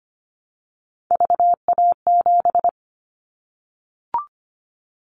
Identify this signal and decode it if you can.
Morse code, heard as a keyed tone, about 25 wpm, 700 Hz. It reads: VA7
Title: Callsign Prefixes for Top 50 DXCC - 25wpm